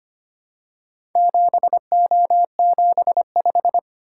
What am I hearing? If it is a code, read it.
Morse code, heard as a keyed tone, about 25 wpm, 700 Hz. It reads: 7O75
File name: practice / 25wpm / Four Letter-Number Combinations - 25wpm